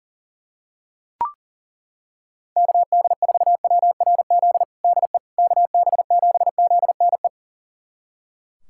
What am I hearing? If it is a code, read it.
Morse code, heard as a keyed tone, about 40 wpm, 700 Hz. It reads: KD4WRZ DE KB7ZNE
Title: USA Callsign DE Callsign - 6 Characters - 40wpm